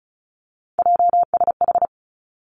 Morse code, heard as a keyed tone, about 36 wpm, 700 Hz. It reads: JSH
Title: Three Challenging Letter-Number Combinations 35wpm